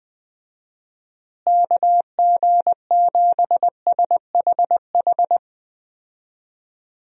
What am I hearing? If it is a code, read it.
Morse code, heard as a keyed tone, about 20 wpm, 700 Hz. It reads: KG7SHH
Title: USA Call Signs - 6 Characters - Repeated 3x - 20wpm